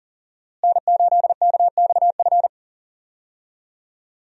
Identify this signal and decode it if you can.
Morse code, heard as a keyed tone, about 40 wpm, 700 Hz. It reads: N8KXF
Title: USA Call Signs - 5 Characters - Repeated 3x - 40wpm